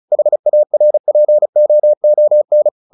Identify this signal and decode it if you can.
Morse code, heard as a keyed tone, about 34 wpm, 600 Hz. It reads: HARPOON
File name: the old man and the sea 2 35wpm